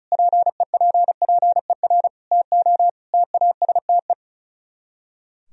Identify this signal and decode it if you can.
Morse code, heard as a keyed tone, about 46 wpm, 700 Hz. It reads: PEPPER TO TASTE